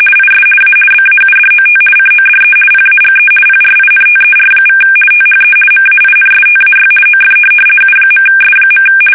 RTTY, about 50 baud. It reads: IS A BAUDOT CODE PAGE PRINTER= THE MAINSTAY OF U.S.
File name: fsk850-50B / teletype-corp